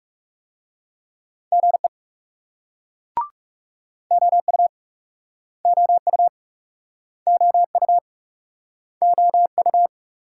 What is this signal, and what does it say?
Morse code, heard as a keyed tone, about 41 wpm, 700 Hz. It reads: GE OU OU OU OU